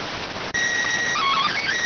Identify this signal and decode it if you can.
SSTV (slow-scan television), PD 290.